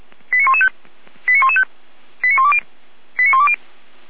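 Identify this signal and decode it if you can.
Five-tone selective call (ZVEI), heard as quick runs of five tones, tones of about 70 ms. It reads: 89115 89115 89120 89120
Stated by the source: ZVEI five-tone selective call 89340 received with an SDR.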